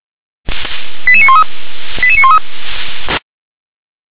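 Five-tone selective call (ZVEI), heard as quick runs of five tones, tones of about 70 ms. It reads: 88913 88913
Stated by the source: Five-tone ZVEI selective call received by SDR, address 88640.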